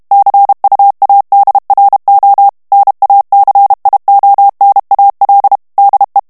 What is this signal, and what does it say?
Morse code, heard as a keyed tone, about 32 wpm, 800 Hz. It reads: CUADRO NACIONAL DE
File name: ESPAGNOL TEXTE EN CLAIR 2 VITESSE 30 MOTS MINUTES